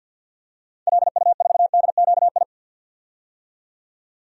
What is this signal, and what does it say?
Morse code, heard as a keyed tone, about 52 wpm, 700 Hz. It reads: LU4DQI
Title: DXCC-100 Call Signs - 6 Characters - 50wpm